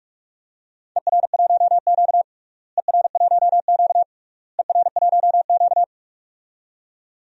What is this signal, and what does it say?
Morse code, heard as a keyed tone, about 44 wpm, 700 Hz. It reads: ER1Q ER1Q ER1Q
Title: DXCC-100 Call Signs - 4 Characters - Repeated 3x - 45wpm